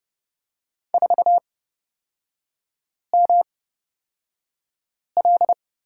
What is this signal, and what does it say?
Morse code, heard as a keyed tone, about 30 wpm, spaced as for 5 wpm, 700 Hz. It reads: 4ML
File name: Lesson 38e - Letter q - Three Characters 30wpm